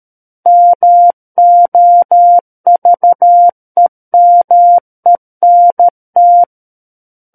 Morse code, heard as a keyed tone, about 13 wpm, 700 Hz. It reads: MOVEMENT
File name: Book One / 04-Chapter 4 - 13 WPM